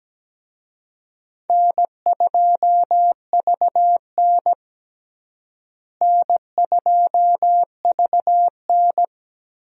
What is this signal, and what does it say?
Morse code, heard as a keyed tone, about 17 wpm, 700 Hz. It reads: N2VN N2VN
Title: USA Call Signs - 4 Characters - Repeated 2x - 17wpm